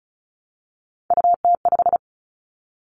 Morse code, heard as a keyed tone, about 34 wpm, 700 Hz. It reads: UT5
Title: Callsign Prefixes for Top 25 DXCC - Repeated 3x - 35wpm